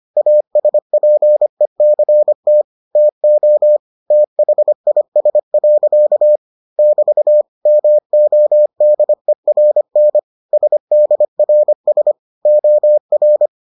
Morse code, heard as a keyed tone, about 25 wpm, 600 Hz. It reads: ASPECT TO THIS. <BT> MODERN SDRS OR